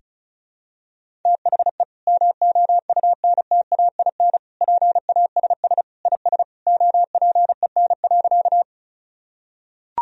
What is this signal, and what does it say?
Morse code, heard as a keyed tone, about 34 wpm, 700 Hz. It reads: THE MOUNTAIN PASS IS OPEN.
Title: Sentences from Top 1000 Words - Review Part 1 - 35wpm